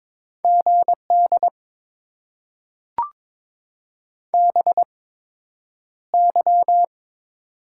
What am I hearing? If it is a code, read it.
Morse code, heard as a keyed tone, about 22 wpm, spaced as for 9 wpm, 700 Hz. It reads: GD BY
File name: Top 90 QSO Elements - 8x Character Spacing - 22wpm